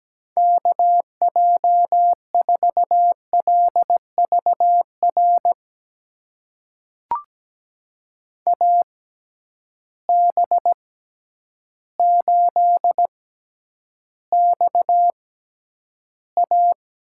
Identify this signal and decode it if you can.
Morse code, heard as a keyed tone, about 17 wpm, spaced as for 9 wpm, 700 Hz. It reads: KJ4LVR AB8XA
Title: USA Call Signs - 6x Character Spacing - 17wpm